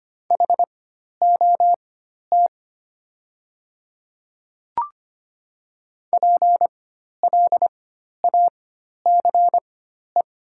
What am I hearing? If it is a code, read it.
Morse code, heard as a keyed tone, about 25 wpm, spaced as for 12 wpm, 700 Hz. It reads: HOT PLACE
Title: 4x Character Spacing - Top 200 Words - 25wpm